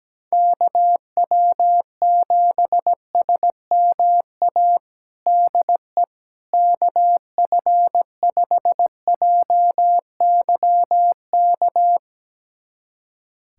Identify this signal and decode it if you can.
Morse code, heard as a keyed tone, about 17 wpm, 700 Hz. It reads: KW7SMA DE KF5JYK
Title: USA Callsign DE Callsign 17wpm